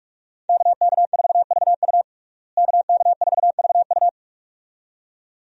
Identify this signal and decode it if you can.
Morse code, heard as a keyed tone, about 46 wpm, 700 Hz. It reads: KK4VU KK4VU
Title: USA Call Signs - 5 Characters - Repeated 2x - 45wpm